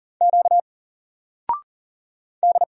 Morse code, heard as a keyed tone, about 39 wpm, 700 Hz. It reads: Q D
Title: Single Letter-Number 40wpm